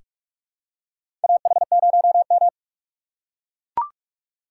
Morse code, heard as a keyed tone, about 44 wpm, 700 Hz. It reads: AH0M
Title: USA Call Signs - 4 Characters - 6x Character Spacing - 45wpm